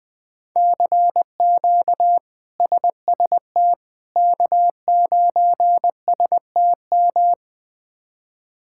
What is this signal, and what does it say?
Morse code, heard as a keyed tone, about 20 wpm, 700 Hz. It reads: CQ SST K9STM
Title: K1USN SST QSO Practice - 20wpm